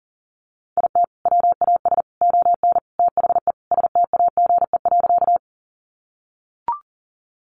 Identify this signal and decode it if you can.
Morse code, heard as a keyed tone, about 40 wpm, 700 Hz. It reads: IT WAS ON THE STAGE.
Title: Sentences from Top 1200 Words 40wpm